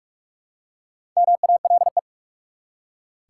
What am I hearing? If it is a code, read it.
Morse code, heard as a keyed tone, about 46 wpm, 700 Hz. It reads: MALE
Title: Four Letter Words - 45wpm